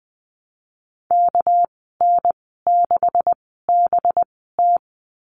Morse code, heard as a keyed tone, about 20 wpm, spaced as for 15 wpm, 700 Hz. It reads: KN6BT
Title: 207 Lesson 33g - Letter j - Call Signs 20wpm